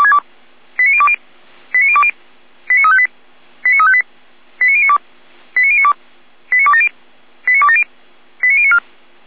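Five-tone selective call (ZVEI), heard as quick runs of five tones, tones of about 70 ms. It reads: 89020 89020 89358 89358 89092 89092 89270 89270 89094
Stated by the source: ZVEI five-tone call received by SDR.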